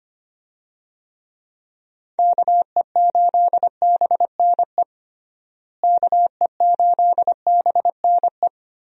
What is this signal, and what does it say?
Morse code, heard as a keyed tone, about 25 wpm, 700 Hz. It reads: KE8BNE KE8BNE
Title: USA Call Signs - 6 Characters - Repeated 2x - 25wpm